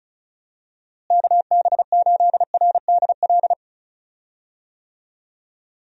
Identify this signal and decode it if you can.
Morse code, heard as a keyed tone, about 34 wpm, 700 Hz. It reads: KB8RDL
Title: USA Call Signs - 6 Characters - Repeated 2x - 35wpm